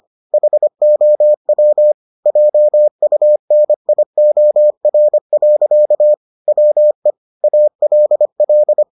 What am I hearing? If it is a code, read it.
Morse code, heard as a keyed tone, about 25 wpm, 600 Hz. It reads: HOW JUNIOR. WE ALL